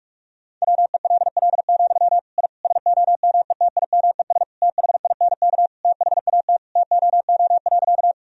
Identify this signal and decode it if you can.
Morse code, heard as a keyed tone, about 41 wpm, 700 Hz. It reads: WELL, I SOMETIMES THINK THAT TOO.
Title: Sentences from Top 1000 Words - Review Part 3 - 45wpm